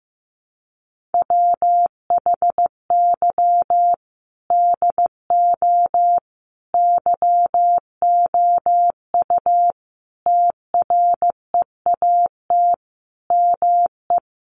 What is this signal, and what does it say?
Morse code, heard as a keyed tone, about 15 wpm, 700 Hz. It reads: WHY DO YOU TREAT ME
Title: Sentences from Top 1300 Words 15wpm